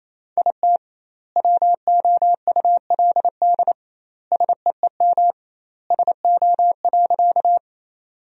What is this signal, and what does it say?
Morse code, heard as a keyed tone, about 28 wpm, 700 Hz. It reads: IT WOULD SEEM SO.
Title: 2x Word-Spacing - Sentences from Top 300 Words - 28wpm